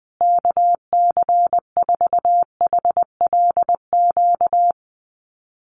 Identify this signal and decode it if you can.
Morse code, heard as a keyed tone, about 20 wpm, 700 Hz. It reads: KC4HLQ
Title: USA Call Signs Repeated 3x 20wpm